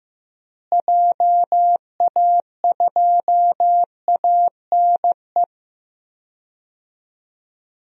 Morse code, heard as a keyed tone, about 15 wpm, 700 Hz. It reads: JA2ANE
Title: DXCC-100 Call Signs - 6 Characters - Repeated 3x - 15wpm